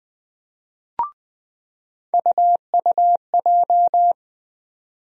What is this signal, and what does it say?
Morse code, heard as a keyed tone, about 20 wpm, 700 Hz. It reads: UUJ
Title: USA Call Sign Postfixes - 3 Letters 20wpm